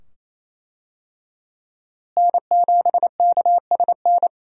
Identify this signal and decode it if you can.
Morse code, heard as a keyed tone, about 28 wpm, 700 Hz. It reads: N7KSN
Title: USA Call Signs - 5 Characters - Repeated 2x - 28wpm